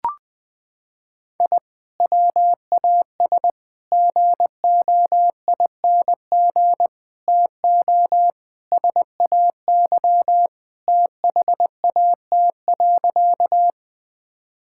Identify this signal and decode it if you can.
Morse code, heard as a keyed tone, about 21 wpm, 700 Hz. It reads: I WAS GOING TO SAY THAT.
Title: Sentences with 5-Word Phrases - 20wpm